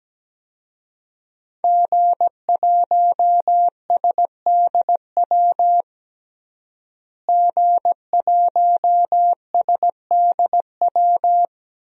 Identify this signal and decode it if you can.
Morse code, heard as a keyed tone, about 17 wpm, 700 Hz. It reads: G1SDW G1SDW
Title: DXCC-100 Call Signs - 5 Characters - Repeated 3x - 17wpm